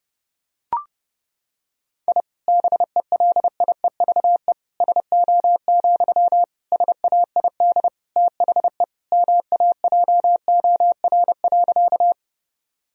Morse code, heard as a keyed tone, about 30 wpm, 700 Hz. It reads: I BELIEVE SO, SAID THE MAJOR.
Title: Sentences from Top 1000 Words - Review Part 4 - 30wpm